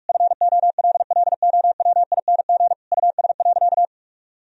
Morse code, heard as a keyed tone, about 46 wpm, 700 Hz. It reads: FOLLOWING US.
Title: Sentences from Top 1000 Words - Review Part 4 - 45wpm